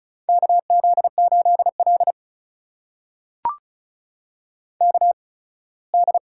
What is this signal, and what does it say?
Morse code, heard as a keyed tone, about 36 wpm, spaced as for 21 wpm, 700 Hz. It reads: KZ8L KD
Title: USA Call Signs - 8x Character Spacing - 35wpm